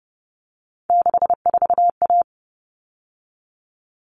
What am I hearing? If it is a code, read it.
Morse code, heard as a keyed tone, about 30 wpm, 700 Hz. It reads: 64A
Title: All Cut Numbers - 3-Digit 30wpm